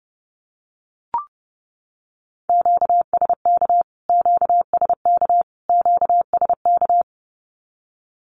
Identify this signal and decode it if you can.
Morse code, heard as a keyed tone, about 30 wpm, 700 Hz. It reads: QSK QSK QSK
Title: Q-Codes Repeated 3x 30wpm